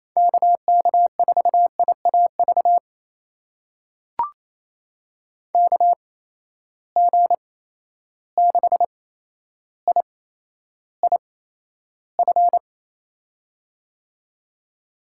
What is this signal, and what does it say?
Morse code, heard as a keyed tone, about 28 wpm, spaced as for 12 wpm, 700 Hz. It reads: KK4IAV KG6IIF
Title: USA Call Signs - 6 Characters - 8x Character Spacing - 28wpm